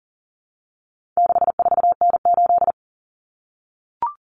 Morse code, heard as a keyed tone, about 40 wpm, 700 Hz. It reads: 64N8
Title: Common Cut Numbers - 4-Digit 40wpm